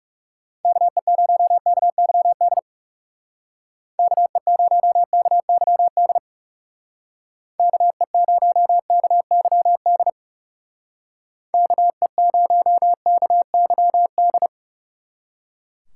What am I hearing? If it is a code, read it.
Morse code, heard as a keyed tone, about 37 wpm, 700 Hz. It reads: KE0KYD KE0KYD KE0KYD KE0KYD
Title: Speed Racing - USA 2x3 Call Signs 30wpm